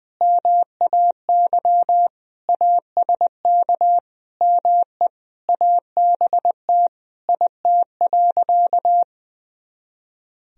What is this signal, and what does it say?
Morse code, heard as a keyed tone, about 21 wpm, 700 Hz. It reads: MAY ASK ME ABT IT.
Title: Sentences from Top 200 Words and Abbreviations 20wpm